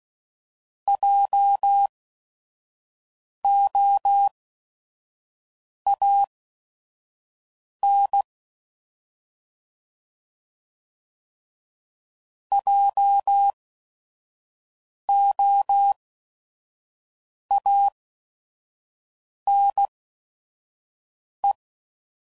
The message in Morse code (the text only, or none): JOAN JOANE